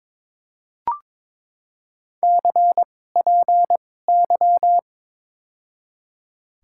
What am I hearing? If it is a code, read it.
Morse code, heard as a keyed tone, about 22 wpm, spaced as for 16 wpm, 700 Hz. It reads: CPY
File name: Top 90 QSO Elements - 2x Character Spacing - 22wpm